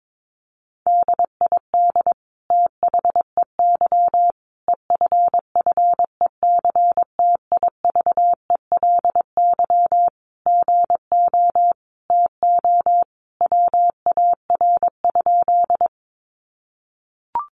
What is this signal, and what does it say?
Morse code, heard as a keyed tone, about 22 wpm, 700 Hz. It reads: DID THEY EFFECTIVELY GO TO WAR?
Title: Sentences from Top 1400 Words 22wpm